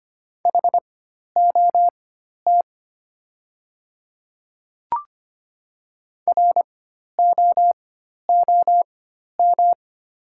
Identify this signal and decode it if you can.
Morse code, heard as a keyed tone, about 25 wpm, spaced as for 12 wpm, 700 Hz. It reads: HOT ROOM
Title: 4x Character Spacing - Top 300 Words - 25wpm